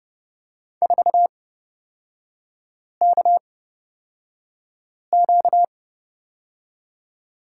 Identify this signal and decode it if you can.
Morse code, heard as a keyed tone, about 31 wpm, spaced as for 5 wpm, 700 Hz. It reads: 4KQ
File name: Lesson 38e - Letter q - Three Characters 30wpm